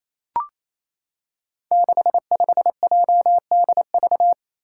Morse code, heard as a keyed tone, about 28 wpm, 700 Hz. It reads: 65JDV
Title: Five Challenging Letter-Number Combinations 28wpm